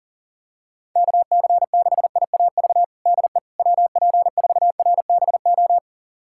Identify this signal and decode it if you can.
Morse code, heard as a keyed tone, about 40 wpm, 700 Hz. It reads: KC6IAV DE WP4RBO
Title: USA Callsign DE Callsign 40wpm